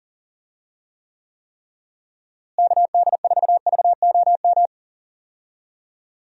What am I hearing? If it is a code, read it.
Morse code, heard as a keyed tone, about 40 wpm, 700 Hz. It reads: KD4VOM
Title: USA Call Signs - 6 Characters - Repeated 3x - 40wpm